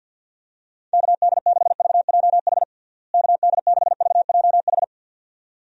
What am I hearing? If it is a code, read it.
Morse code, heard as a keyed tone, about 52 wpm, 700 Hz. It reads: KD6VJH KD6VJH
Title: USA Call Signs - 6 Characters - Repeated 2x - 50wpm